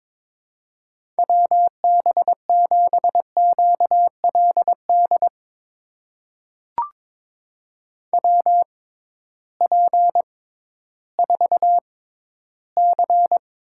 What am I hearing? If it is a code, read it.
Morse code, heard as a keyed tone, about 22 wpm, spaced as for 13 wpm, 700 Hz. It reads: WB7QLD WP4C
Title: USA Call Signs - 6 Characters - 6x Character Spacing - 22wpm